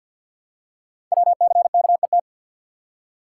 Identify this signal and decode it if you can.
Morse code, heard as a keyed tone, about 51 wpm, 700 Hz. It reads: WCKET
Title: Code Groups - 5 Letters - 50wpm